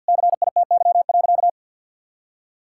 Morse code, heard as a keyed tone, about 50 wpm, 700 Hz. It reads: CITY.
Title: Sentences from Top 1600 Words 50wpm